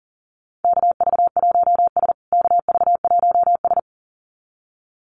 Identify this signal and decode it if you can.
Morse code, heard as a keyed tone, about 40 wpm, 700 Hz. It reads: KV1S KV1S
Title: USA Call Signs Repeated 2x 40wpm